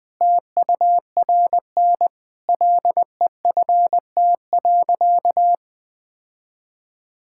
Morse code, heard as a keyed tone, about 20 wpm, 700 Hz. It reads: TURN LEFT.
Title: Sentences with 5-Word Phrases - Spoken Beginning - 20wpm